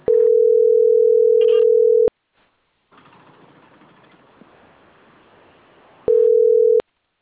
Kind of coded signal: telephone call progress tone